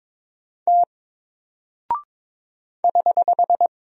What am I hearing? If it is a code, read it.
Morse code, heard as a keyed tone, about 22 wpm, 700 Hz. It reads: T <HH>
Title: Single Character (Letter:Number:Punctuation:Prosign) Alt 22wpm